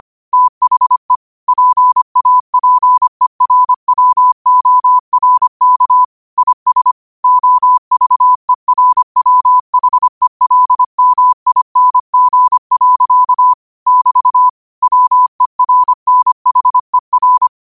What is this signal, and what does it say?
Morse code, heard as a keyed tone, about 25 wpm, 1000 Hz. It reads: THE PAPERWORK IS OVERWHELMING. <BT> WERNHER